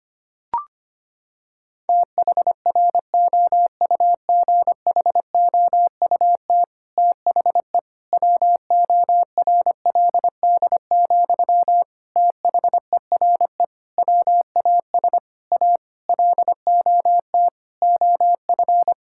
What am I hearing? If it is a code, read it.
Morse code, heard as a keyed tone, about 25 wpm, 700 Hz. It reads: THROUGHOUT THE WORLD, THERE WAS A LOT OF
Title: Sentences from Top 1000 Words - Review Part 5 - 25wpm